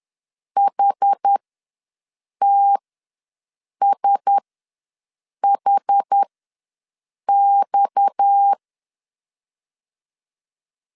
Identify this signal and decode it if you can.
Morse code, heard as a keyed tone, about 10 wpm, spaced as for 6 wpm, 800 Hz. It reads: HTSHX
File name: LEZIONI / 6 PKX0 / 6 PKX ws 10wpm 3706